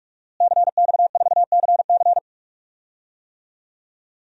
Morse code, heard as a keyed tone, about 46 wpm, 700 Hz. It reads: CX4CC
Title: DXCC-100 Call Signs - 5 Characters - Repeated 2x - 45wpm